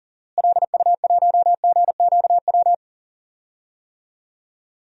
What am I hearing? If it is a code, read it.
Morse code, heard as a keyed tone, about 40 wpm, 700 Hz. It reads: LU1GQW
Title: DXCC-100 Call Signs - 6 Characters - Repeated 2x - 40wpm